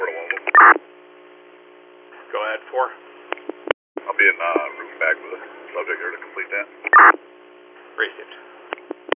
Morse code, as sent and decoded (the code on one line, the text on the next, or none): none